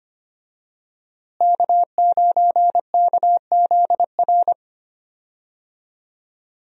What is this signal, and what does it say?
Morse code, heard as a keyed tone, about 25 wpm, 700 Hz. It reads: K9KZR
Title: USA Call Signs - 5 Characters - Repeated 2x - 25wpm